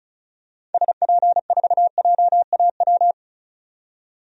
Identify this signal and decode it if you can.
Morse code, heard as a keyed tone, about 34 wpm, 700 Hz. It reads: SP4JAW
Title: DXCC-100 Call Signs - 6 Characters - 35wpm